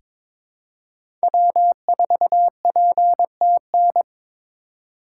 Morse code, heard as a keyed tone, about 22 wpm, 700 Hz. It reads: W4PTN
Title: USA Call Signs - 6x Character Spacing - 22wpm